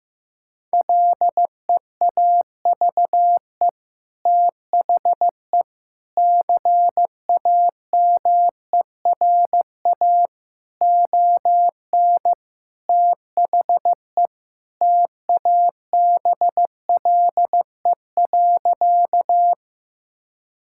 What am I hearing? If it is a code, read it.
Morse code, heard as a keyed tone, about 15 wpm, 700 Hz. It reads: LEAVE THE CAMERA ON THE TABLE.
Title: Sentences from Top 1000 Words - Review Part 5 - 15wpm